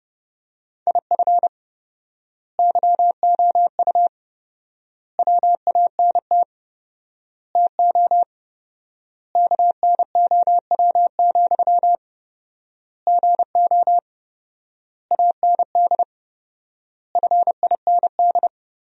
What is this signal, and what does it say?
Morse code, heard as a keyed tone, about 32 wpm, 700 Hz. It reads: IF YOU WANT TO KNOW, GO AND FIND
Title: 4x Word-Spacing - Sentences from Top 200 Words - 30wpm